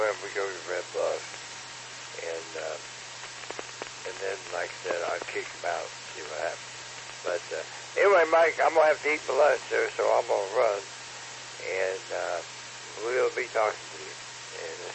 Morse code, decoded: none

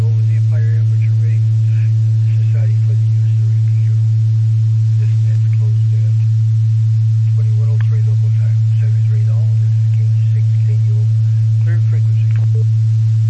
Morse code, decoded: none

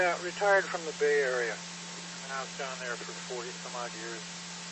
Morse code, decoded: none

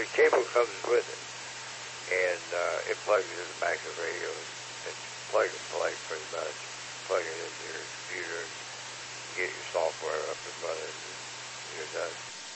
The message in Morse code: none

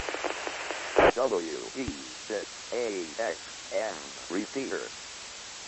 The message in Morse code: none

